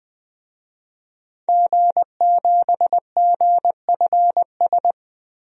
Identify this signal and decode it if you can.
Morse code, heard as a keyed tone, about 20 wpm, 700 Hz. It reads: G7GFS
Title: DXCC-100 Call Signs - 5 Characters - Repeated 2x - 20wpm